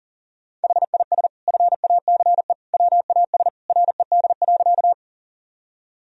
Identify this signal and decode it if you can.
Morse code, heard as a keyed tone, about 40 wpm, 700 Hz. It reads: HIS FACE WAS RED.
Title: Sentences from Top 1000 Words - Review Part 2 - 40wpm